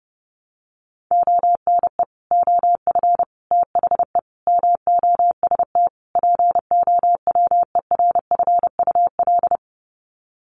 Morse code, heard as a keyed tone, about 30 wpm, 700 Hz. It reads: ONE OF THE MOST POWERFUL
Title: Sets of 5 Words Encore 30wpm